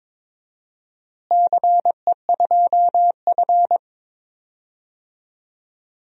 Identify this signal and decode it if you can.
Morse code, heard as a keyed tone, about 21 wpm, 700 Hz. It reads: CE2F
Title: DXCC-100 Call Signs - 4 Characters - Repeated 3x - 22wpm